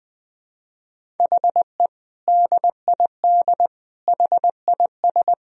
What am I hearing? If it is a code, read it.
Morse code, heard as a keyed tone, about 20 wpm, 700 Hz. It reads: HE DID HIS
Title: Sentences from Top 1000 Words - Review Part 1 - 20wpm